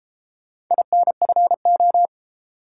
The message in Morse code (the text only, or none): INFO